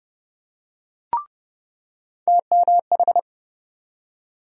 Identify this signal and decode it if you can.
Morse code, heard as a keyed tone, about 30 wpm, 700 Hz. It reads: TMH
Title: USA Call Sign Postfixes - 3 Letters 30wpm